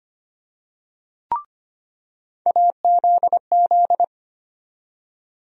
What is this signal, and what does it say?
Morse code, heard as a keyed tone, about 25 wpm, 700 Hz. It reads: AZZ
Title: USA Call Sign Postfixes - 3 Letters 25wpm